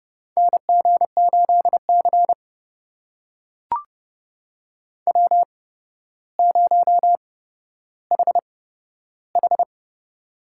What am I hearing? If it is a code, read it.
Morse code, heard as a keyed tone, about 30 wpm, spaced as for 13 wpm, 700 Hz. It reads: NG8C W0HH
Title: USA Call Signs - 4 Characters - 8x Character Spacing - 30wpm